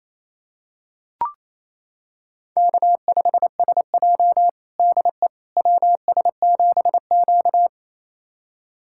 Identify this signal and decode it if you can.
Morse code, heard as a keyed tone, about 28 wpm, 700 Hz. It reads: K5SJ DE WS7Q
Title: USA Callsign DE Callsign - 4 Characters - 28wpm